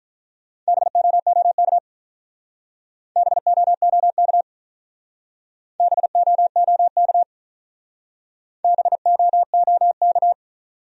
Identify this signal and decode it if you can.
Morse code, heard as a keyed tone, about 42 wpm, 700 Hz. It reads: BOOK BOOK BOOK BOOK
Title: Speed-Racing Top 500 Words 35wpm